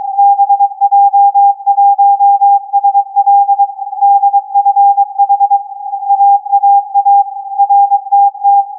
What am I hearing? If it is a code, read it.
Morse code, heard as a keyed tone, about 22 wpm, 800 Hz. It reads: BJ1SL DFH AAA RTT
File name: CAS4B-TURN-AIRSPY-01